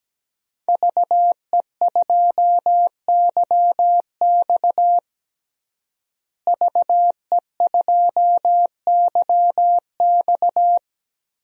Morse code, heard as a keyed tone, about 17 wpm, 700 Hz. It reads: VE2YX VE2YX
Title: DXCC-100 Call Signs - 5 Characters - Repeated 2x - 17wpm